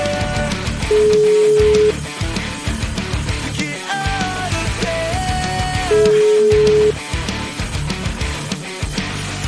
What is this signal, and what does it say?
Ringback tone (Europe, 425 Hz), with noise.